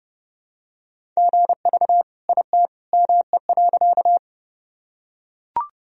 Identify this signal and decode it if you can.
Morse code, heard as a keyed tone, about 30 wpm, 700 Hz. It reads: GV IT ME.
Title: Sentences from Top 200 Words and Abbreviations 30wpm